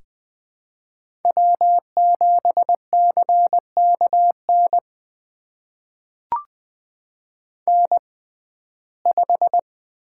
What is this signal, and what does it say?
Morse code, heard as a keyed tone, about 20 wpm, spaced as for 14 wpm, 700 Hz. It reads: W7CKN N5
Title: USA Call Signs - 5 Characters - 6x Character Spacing - 20wpm